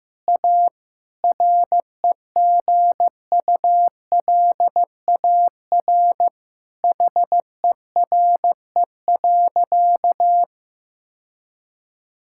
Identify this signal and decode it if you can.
Morse code, heard as a keyed tone, about 15 wpm, 700 Hz. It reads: A REGULAR HERE.
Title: Sentences from Top 1000 Words - Review Part 6 - 15wpm